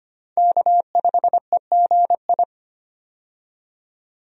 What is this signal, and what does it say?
Morse code, heard as a keyed tone, about 25 wpm, 690 Hz. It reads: K5EGI
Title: USA Call Signs - 5 Characters - 25wpm